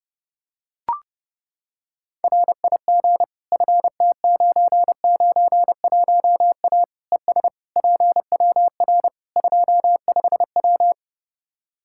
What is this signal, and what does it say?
Morse code, heard as a keyed tone, about 30 wpm, 700 Hz. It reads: RIG FT991A ES PWR 25W
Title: Practice QSO - Exchange 2 - Info - 30wpm